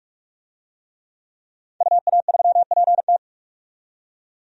Morse code, heard as a keyed tone, about 46 wpm, 700 Hz. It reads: UA3PT
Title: DXCC-100 Call Signs - 5 Characters - 45wpm